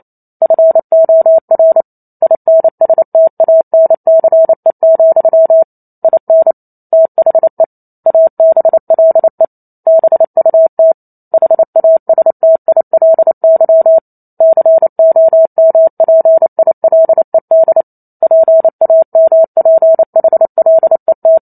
Morse code, heard as a keyed tone, about 28 wpm, 650 Hz. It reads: FOR INSTANCE, IN THE ABLE BUT HASTILY COMPILED PAMPHLET